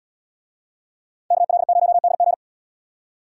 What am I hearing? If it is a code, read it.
Morse code, heard as a keyed tone, about 77 wpm, 700 Hz. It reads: KL0DG